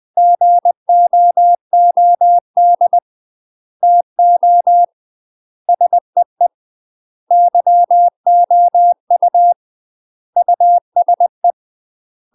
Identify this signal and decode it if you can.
Morse code, heard as a keyed tone, about 20 wpm, 700 Hz. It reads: GOOD TO SEE YOU USE
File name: UK MASTER QSOs TRAINING / 20WPM / UK MASTER QSOs 0067